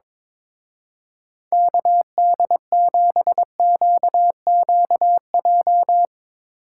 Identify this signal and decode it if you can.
Morse code, heard as a keyed tone, about 24 wpm, 700 Hz. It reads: KD7QQJ
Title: Speed Racing - USA Call Signs 22wpm